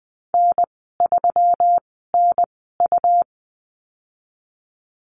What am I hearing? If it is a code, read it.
Morse code, heard as a keyed tone, about 19 wpm, spaced as for 14 wpm, 700 Hz. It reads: N3NU
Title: 147 Lesson 24g - Letter y - Call Signs 20wpm